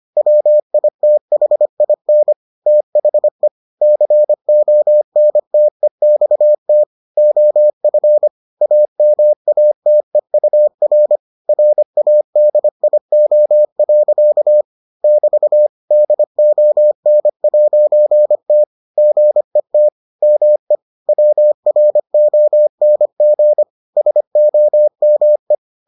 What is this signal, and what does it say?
Morse code, heard as a keyed tone, about 25 wpm, 600 Hz. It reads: WITHIN THE CONTEXT OF AMATEUR RADIO. <BT> DON'T GET ME WRONG SOME